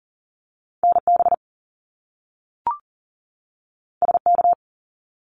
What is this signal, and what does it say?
Morse code, heard as a keyed tone, about 40 wpm, 700 Hz. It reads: NB SK
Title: Canadian Provinces and Territories 40wpm